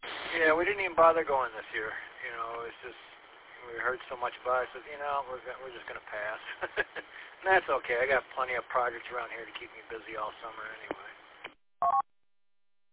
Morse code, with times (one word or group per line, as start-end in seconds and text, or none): none